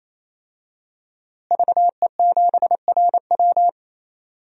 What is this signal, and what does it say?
Morse code, heard as a keyed tone, about 28 wpm, 700 Hz. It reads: VE7RW
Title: DXCC-100 Call Signs - 5 Characters - Repeated 3x - 28wpm